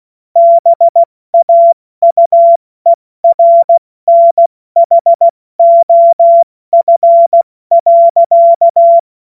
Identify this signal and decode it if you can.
Morse code, heard as a keyed tone, about 16 wpm, 680 Hz. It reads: BAUERNHOF.
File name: Witz-16-September-2025-70-F80